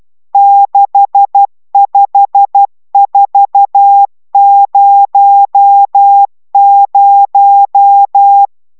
Morse code, heard as a keyed tone, about 12 wpm, 800 Hz. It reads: 65400
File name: taviro tanulas / cw 69 20 10